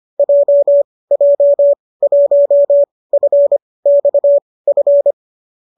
Morse code, heard as a keyed tone, about 25 wpm, spaced as for 18 wpm, 570 Hz. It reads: JJ1FXF